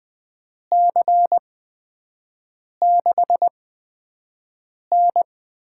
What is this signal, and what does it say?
Morse code, heard as a keyed tone, about 20 wpm, spaced as for 5 wpm, 700 Hz. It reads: C6N
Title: USA Call Signs - 8x Character Spacing - 20wpm